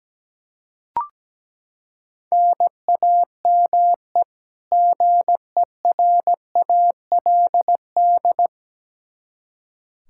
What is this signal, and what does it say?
Morse code, heard as a keyed tone, about 17 wpm, 700 Hz. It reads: NAME GERALD
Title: Practice QSO - Exchange 1 - Beginning - 17wpm